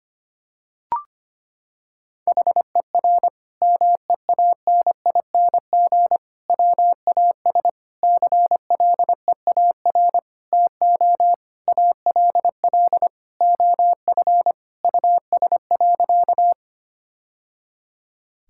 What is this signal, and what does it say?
Morse code, heard as a keyed tone, about 25 wpm, 700 Hz. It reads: HER MEANING WAS CLEAR TO ALL OF US.
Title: Sentences from Top 1000 Words - Review Part 6 - 25wpm